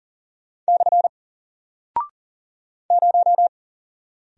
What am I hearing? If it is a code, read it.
Morse code, heard as a keyed tone, about 40 wpm, 700 Hz. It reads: / 0
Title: Single Character (Letter:Number:Punctuation:Prosign) Alt 40wpm